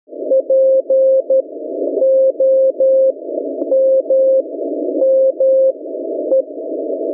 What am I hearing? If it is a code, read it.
Morse code, heard as a keyed tone, about 12 wpm, spaced as for 9 wpm, 530 Hz. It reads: POMME